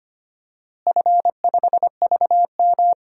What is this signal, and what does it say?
Morse code, heard as a keyed tone, about 25 wpm, 700 Hz. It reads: F5VM
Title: DXCC-100 Call Signs - 4 Characters - Repeated 2x - 25wpm